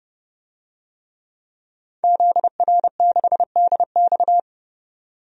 Morse code, heard as a keyed tone, about 30 wpm, 700 Hz. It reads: ZR6DX
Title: DXCC-100 Call Signs - 5 Characters - 30wpm